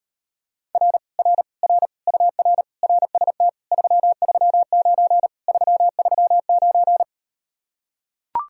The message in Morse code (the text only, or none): R R R UR RST 339 339